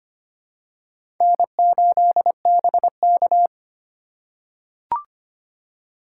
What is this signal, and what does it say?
Morse code, heard as a keyed tone, about 25 wpm, 700 Hz. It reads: N8BK
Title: USA Call Signs - 4 Characters - 4x Character Spacing - 25wpm